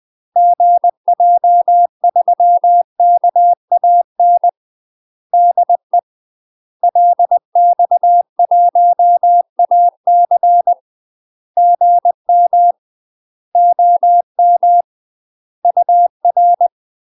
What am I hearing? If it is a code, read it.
Morse code, heard as a keyed tone, about 20 wpm, 700 Hz. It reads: GJ3KAN DE LX1AC GM OM UR
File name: UK MASTER QSOs TRAINING / 20WPM / UK MASTER QSOs 0036